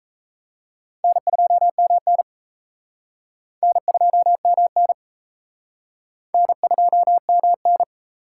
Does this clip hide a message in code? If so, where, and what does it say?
Yes: Morse code from 1.0 to 7.8 s, heard as a keyed tone, about 37 wpm, 700 Hz. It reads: N2MN N2MN N2MN